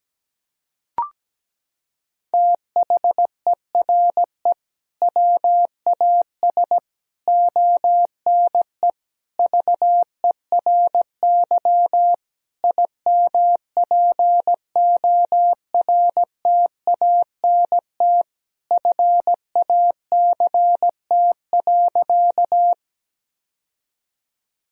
Morse code, heard as a keyed tone, about 17 wpm, 700 Hz. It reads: THERE WAS ONE VERY IMPORTANT FACT.
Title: Sentences from Top 1000 Words - Review Part 3 - 17wpm